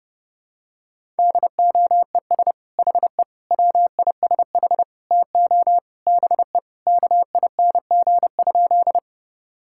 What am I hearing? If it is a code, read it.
Morse code, heard as a keyed tone, about 32 wpm, 700 Hz. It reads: DOES HE WISH TO BE KING?
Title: Sentences from Top 1600 Words 30wpm